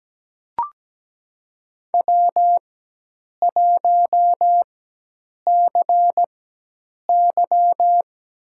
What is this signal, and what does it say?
Morse code, heard as a keyed tone, about 17 wpm, spaced as for 8 wpm, 700 Hz. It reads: W1CY
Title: USA Call Signs - 4 Characters - 4x Character Spacing - 17wpm